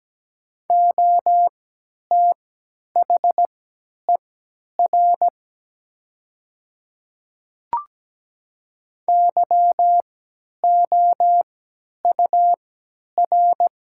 Morse code, heard as a keyed tone, about 17 wpm, spaced as for 10 wpm, 700 Hz. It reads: OTHER YOUR
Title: 3x Character Spacing - Top 200 Words - 17wpm